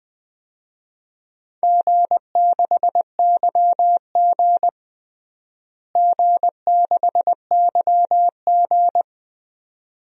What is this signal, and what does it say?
Morse code, heard as a keyed tone, about 21 wpm, 700 Hz. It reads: G6YG G6YG
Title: DXCC-100 Call Signs - 4 Characters - Repeated 2x - 20wpm